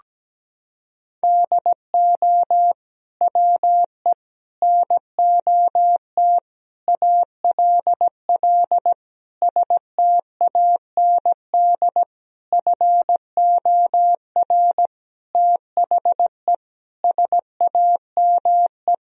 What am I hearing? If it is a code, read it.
Morse code, heard as a keyed tone, about 17 wpm, 700 Hz. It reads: DO WE NOT ALL STAND FOR THE SAME
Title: Sentences from Top 1000 Words - Review Part 3 - 17wpm